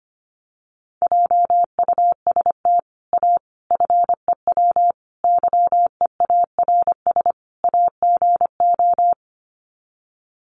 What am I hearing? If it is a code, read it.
Morse code, heard as a keyed tone, about 25 wpm, 700 Hz. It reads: JUST A FEW YEARS AGO
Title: Sets of 5 Words Encore 25wpm